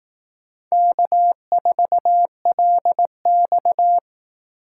Morse code, heard as a keyed tone, about 18 wpm, 700 Hz. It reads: K4LX
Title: Speed Racing - USA 1x2 Call Signs 15wpm